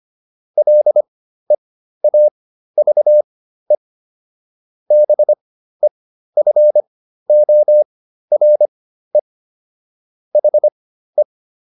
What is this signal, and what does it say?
Morse code, heard as a keyed tone, about 25 wpm, spaced as for 13 wpm, 600 Hz. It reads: LEAVE BEFORE HE